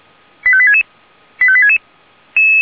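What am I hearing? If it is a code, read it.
Five-tone selective call (ZVEI), heard as quick runs of five tones, tones of about 80 ms. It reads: 85866 85866
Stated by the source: ZVEI five-tone call received on SDR.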